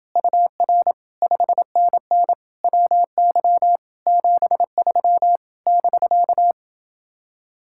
Morse code, heard as a keyed tone, about 28 wpm, 700 Hz. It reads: UR 5NN WY 73 <BK>